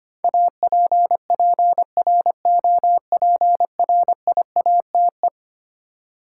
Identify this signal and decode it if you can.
Morse code, heard as a keyed tone, about 25 wpm, 700 Hz. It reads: APPROPRIATE
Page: morse-code-ninja-practice-25wpm